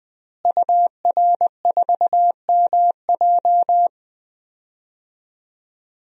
Morse code, heard as a keyed tone, about 20 wpm, 700 Hz. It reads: UR4MJ
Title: DXCC-100 Call Signs - 5 Characters - Repeated 2x - 20wpm